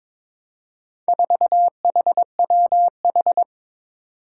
Morse code, heard as a keyed tone, about 22 wpm, 700 Hz. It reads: 4HWH
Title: Four Challenging Letter-Number Combinations 22wpm